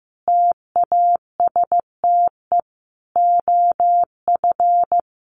Morse code, heard as a keyed tone, about 15 wpm, 700 Hz. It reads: TASTE OF
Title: Sentences from Top 1100 Words 15wpm